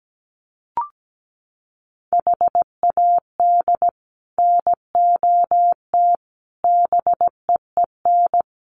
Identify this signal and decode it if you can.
Morse code, heard as a keyed tone, about 17 wpm, 700 Hz. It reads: HAD NOT BEEN
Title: Sets of 5 Words Encore 17wpm